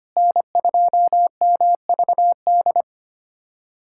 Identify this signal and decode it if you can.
Morse code, heard as a keyed tone, about 25 wpm, 700 Hz. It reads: N2MVD
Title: USA Callsign DE Callsign - 5 Characters - 25wpm